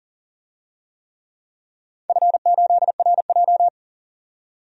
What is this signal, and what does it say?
Morse code, heard as a keyed tone, about 40 wpm, 700 Hz. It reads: F8RJ
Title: DXCC-100 Call Signs - 4 Characters - 40wpm